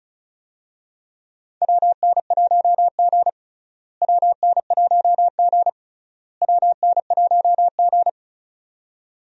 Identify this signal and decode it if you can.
Morse code, heard as a keyed tone, about 36 wpm, 700 Hz. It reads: WN1G WN1G WN1G
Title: USA Call Signs - 4 Characters - Repeated 3x - 35wpm